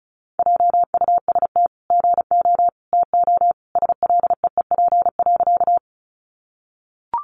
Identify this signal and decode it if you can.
Morse code, heard as a keyed tone, about 34 wpm, 700 Hz. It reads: JUST GO TO SLEEP.
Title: Sentences from Top 500 Words 35wpm